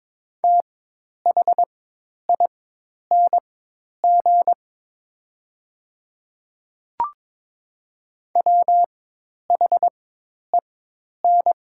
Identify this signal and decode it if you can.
Morse code, heard as a keyed tone, about 22 wpm, spaced as for 10 wpm, 690 Hz. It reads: THING WHEN
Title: 4x Character Spacing - Top 300 Words - 22wpm